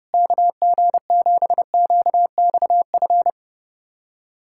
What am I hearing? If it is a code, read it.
Morse code, heard as a keyed tone, about 30 wpm, 700 Hz. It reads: KG7QXF
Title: USA Callsign DE Callsign - 6 Characters - 30wpm